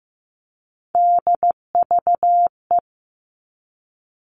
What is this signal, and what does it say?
Morse code, heard as a keyed tone, about 15 wpm, 700 Hz. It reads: DVE
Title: All Cut Numbers - 3-Digit 15wpm